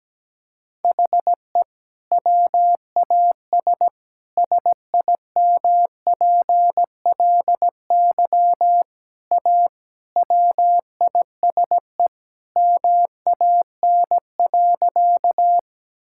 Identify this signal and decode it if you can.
Morse code, heard as a keyed tone, about 17 wpm, 700 Hz. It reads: HE WAS SIMPLY A WISE MAN.
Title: Sentences from Top 1600 Words 17wpm